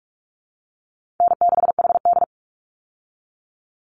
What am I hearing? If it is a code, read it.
Morse code, heard as a keyed tone, about 44 wpm, 700 Hz. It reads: N6HD
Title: Call Signs Repeated 3x - 45wpm